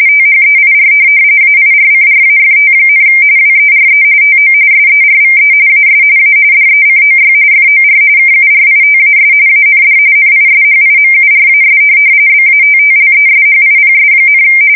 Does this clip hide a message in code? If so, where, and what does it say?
Yes: RTTY from 0.1 to 14.4 s, about 45.45 baud. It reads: THEIR AIRCRAFT, WERE TO SERVE AS A DECOY FOR KURITAS AND NISHIMURAS BATTLESHIPS,